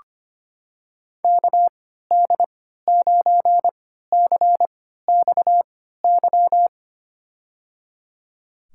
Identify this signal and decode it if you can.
Morse code, heard as a keyed tone, about 25 wpm, spaced as for 14 wpm, 700 Hz. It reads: KD9CXY
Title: USA Call Signs - 6 Characters - 3x Character Spacing - 25wpm